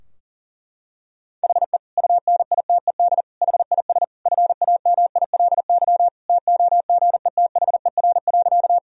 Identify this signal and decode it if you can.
Morse code, heard as a keyed tone, about 40 wpm, 700 Hz. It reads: HE UNITED HIS FAMILY TOGETHER.
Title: Sentences from Top 1000 Words - Review Part 4 - 40wpm